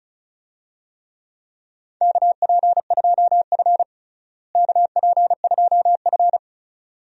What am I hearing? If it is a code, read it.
Morse code, heard as a keyed tone, about 36 wpm, 700 Hz. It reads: KP2F KP2F
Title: DXCC-100 Call Signs - 4 Characters - Repeated 2x - 35wpm